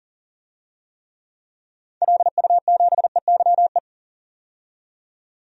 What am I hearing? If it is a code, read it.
Morse code, heard as a keyed tone, about 40 wpm, 700 Hz. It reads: LU7EYE
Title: DXCC-100 Call Signs - 6 Characters - 40wpm